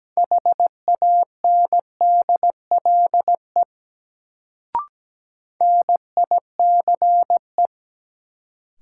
Morse code, heard as a keyed tone, about 17 wpm, 700 Hz. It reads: HANDLE NICE
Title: Sentences from Top 1100 Words - New Word Drill Alt - 17wpm